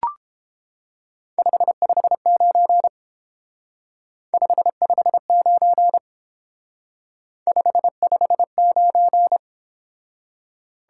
Morse code, heard as a keyed tone, about 29 wpm, 700 Hz. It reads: 559 559 559